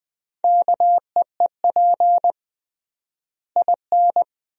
Morse code, heard as a keyed tone, about 20 wpm, 700 Hz. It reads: KEEP IN
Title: 3x Word-Spacing - Sentences from Top 400 Words - 20wpm